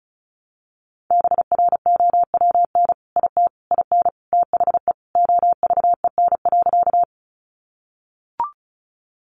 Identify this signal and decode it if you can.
Morse code, heard as a keyed tone, about 34 wpm, 700 Hz. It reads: BROWN IT IN THE OVEN.
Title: Sentences from Top 1200 Words 35wpm